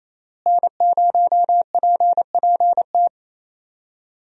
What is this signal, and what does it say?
Morse code, heard as a keyed tone, about 28 wpm, 700 Hz. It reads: N0PPT
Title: USA Call Signs - 5 Characters - Repeated 3x - 28wpm